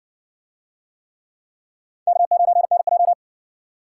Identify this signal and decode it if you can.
Morse code, heard as a keyed tone, about 60 wpm, 700 Hz. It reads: K9NJ